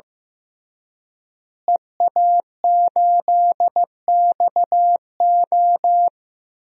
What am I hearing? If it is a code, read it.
Morse code, heard as a keyed tone, about 16 wpm, 700 Hz. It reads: EA8XO
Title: DXCC-100 Call Signs - 5 Characters - Repeated 3x - 15wpm